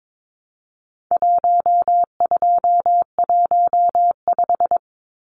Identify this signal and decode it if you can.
Morse code, heard as a keyed tone, about 22 wpm, 700 Hz. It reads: 1215
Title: Numbers - 4-Digit 22wpm